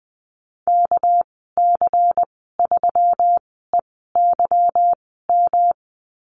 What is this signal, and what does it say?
Morse code, heard as a keyed tone, about 20 wpm, spaced as for 15 wpm, 700 Hz. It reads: KC3EYM
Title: 238 Lesson 38g - Letter q - Call Signs 20wpm